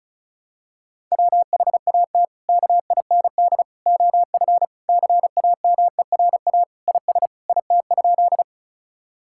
Morse code, heard as a keyed tone, about 34 wpm, 700 Hz. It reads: WHAT KIND OF CAMERA IS IT?
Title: Sentences from Top 1000 Words - Review Part 5 - 35wpm